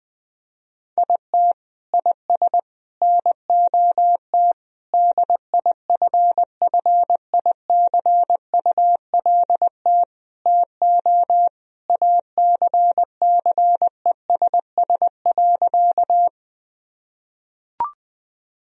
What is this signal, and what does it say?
Morse code, heard as a keyed tone, about 19 wpm, 700 Hz. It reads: IT IS NOT DIFFICULT TO ACCESS.
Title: Sentences from Top 1000 Words - Review Part 4 - 20wpm